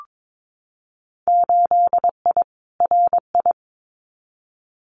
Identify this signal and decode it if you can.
Morse code, heard as a keyed tone, about 22 wpm, 700 Hz. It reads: 8I RI
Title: Winter Field Day Exchanges Encore 22wpm